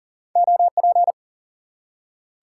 Morse code, heard as a keyed tone, about 39 wpm, 700 Hz. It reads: OP
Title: Top 90 QSO Elements 40wpm